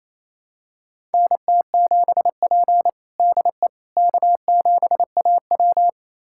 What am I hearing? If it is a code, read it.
Morse code, heard as a keyed tone, about 28 wpm, 700 Hz. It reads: NT7P DE K7AW
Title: USA Callsign DE Callsign - 4 Characters - 28wpm